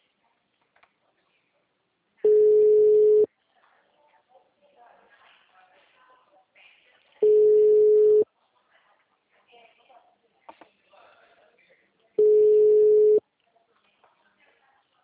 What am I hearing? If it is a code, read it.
Ringback tone (Europe, 425 Hz), with speech.